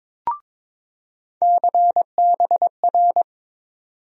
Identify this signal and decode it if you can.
Morse code, heard as a keyed tone, about 22 wpm, 700 Hz. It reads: CBR
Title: USA Call Sign Postfixes - 3 Letters 22wpm